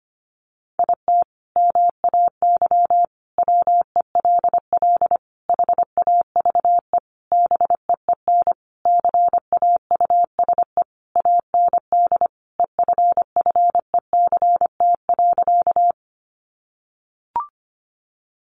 Morse code, heard as a keyed tone, about 25 wpm, 700 Hz. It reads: IT MAY WELL HAVE BEEN CAUSE AND EFFECT.
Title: Sentences from Top 1000 Words 25wpm